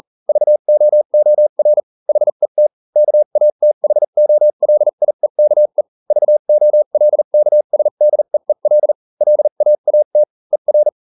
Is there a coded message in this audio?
Yes: Morse code from 0.3 to 10.9 s, heard as a keyed tone, about 40 wpm, spaced as for 35 wpm, 600 Hz. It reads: VOOR HET KATHOLIEKE VOLKSDEEL LAAT ER